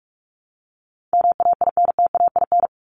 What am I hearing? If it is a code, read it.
Morse code, heard as a keyed tone, about 44 wpm, 700 Hz. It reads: MAINTAIN
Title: Top 501-750 Words 45wpm